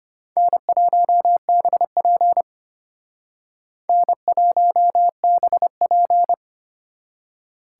Morse code, heard as a keyed tone, about 32 wpm, 700 Hz. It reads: N1BP N1BP